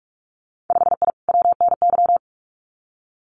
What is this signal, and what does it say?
Morse code, heard as a keyed tone, about 46 wpm, 700 Hz. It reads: 5I WNY
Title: Winter Field Day Exchanges Encore 45wpm